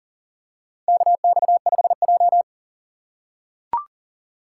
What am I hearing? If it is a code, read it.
Morse code, heard as a keyed tone, about 40 wpm, 700 Hz. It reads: KX5J
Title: USA Call Signs - 4 Characters - 6x Character Spacing - 40wpm